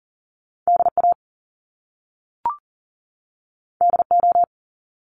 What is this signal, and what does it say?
Morse code, heard as a keyed tone, about 40 wpm, 700 Hz. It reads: DA DO
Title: Top Two Letter Combinations in Words 40wpm